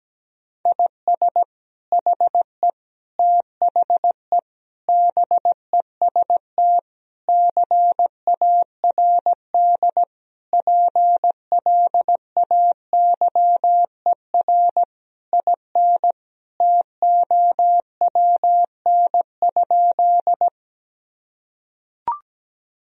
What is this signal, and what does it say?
Morse code, heard as a keyed tone, about 17 wpm, 700 Hz. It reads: IS HE THE BEST CARD PLAYER IN TOWN?
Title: Sentences from Top 1000 Words - Review Part 4 - 17wpm